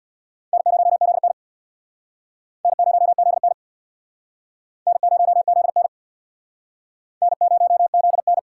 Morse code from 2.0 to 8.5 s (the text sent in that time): N0ZN N0ZN N0ZN